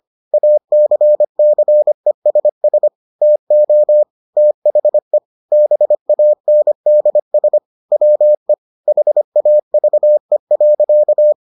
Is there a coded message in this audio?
Yes: Morse code from 0.3 to 11.3 s, heard as a keyed tone, about 25 wpm, 600 Hz. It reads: ACCESS TO THE BANDS WE HAVE.